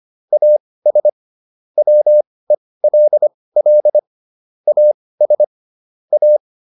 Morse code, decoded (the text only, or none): AS WELL AS A